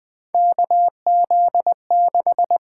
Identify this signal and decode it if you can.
Morse code, heard as a keyed tone, about 20 wpm, 700 Hz. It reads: KZ6
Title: USA Call Signs - 4 Characters - 8x Character Spacing - 20wpm